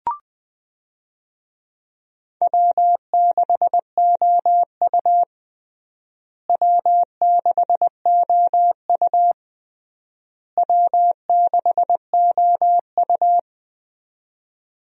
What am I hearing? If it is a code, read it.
Morse code, heard as a keyed tone, about 20 wpm, 700 Hz. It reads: W6OU W6OU W6OU
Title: USA Call Signs - 4 Characters - Repeated 3x - 20wpm